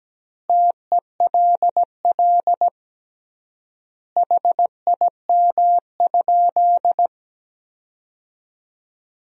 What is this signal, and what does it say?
Morse code, heard as a keyed tone, about 17 wpm, 700 Hz. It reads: TELL HIM?
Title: 3x Word-Spacing - Sentences from Top 200 Words - 17wpm